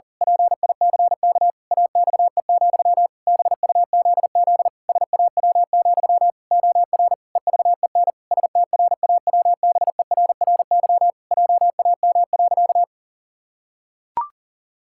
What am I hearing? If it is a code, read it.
Morse code, heard as a keyed tone, about 40 wpm, 700 Hz. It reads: PICK AXE, BUZZ SAW, OR EVEN STRAWBERRY JAM.
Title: Pangram Challenge v2 - 40wpm